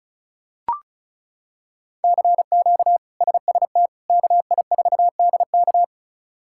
Morse code, heard as a keyed tone, about 34 wpm, 700 Hz. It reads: CQ SST KI4DK
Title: K1USN SST QSO Practice - 35wpm